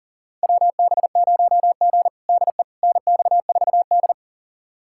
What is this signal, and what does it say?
Morse code, heard as a keyed tone, about 40 wpm, 700 Hz. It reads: WB0G DE NX4D
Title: USA Callsign DE Callsign - 4 Characters - 40wpm